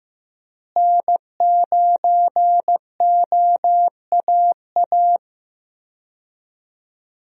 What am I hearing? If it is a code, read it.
Morse code, heard as a keyed tone, about 15 wpm, 700 Hz. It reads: N9OAA
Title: USA Call Signs - 5 Characters - Repeated 2x - 15wpm